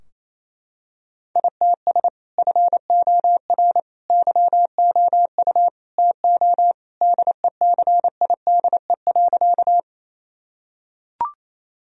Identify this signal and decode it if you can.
Morse code, heard as a keyed tone, about 28 wpm, 700 Hz. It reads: ITS FOR YOU TO DECIDE.
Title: Sentences from Top 1000 Words - Review Part 3 - 28wpm